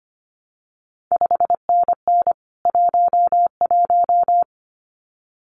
Morse code, heard as a keyed tone, about 25 wpm, 700 Hz. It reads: 5NN 11
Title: CQ WW DX Contest Exchange Encore - 25wpm